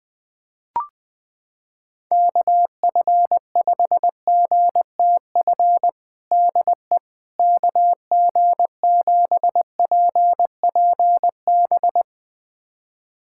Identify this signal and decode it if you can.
Morse code, heard as a keyed tone, about 21 wpm, 700 Hz. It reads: KF5GTF DE KG7PPB
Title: USA Callsign DE Callsign 20wpm